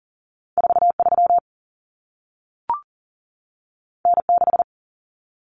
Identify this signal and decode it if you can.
Morse code, heard as a keyed tone, about 40 wpm, 700 Hz. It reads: 43 N6
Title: Common Cut Numbers - 2-Digit 40wpm